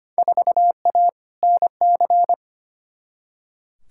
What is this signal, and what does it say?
Morse code, heard as a keyed tone, about 25 wpm, 700 Hz. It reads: 4A NC
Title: Field Day Exchanges - 25wpm